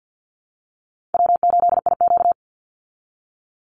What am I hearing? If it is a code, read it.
Morse code, heard as a keyed tone, about 50 wpm, 700 Hz. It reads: W8IQ
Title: Easy Call Signs Repeated 2x - 50wpm